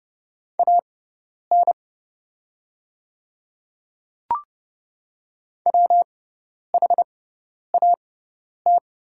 Morse code, heard as a keyed tone, about 30 wpm, spaced as for 10 wpm, 700 Hz. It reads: AN WHAT
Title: 6x Character Spacing - Top 100 Words - 30wpm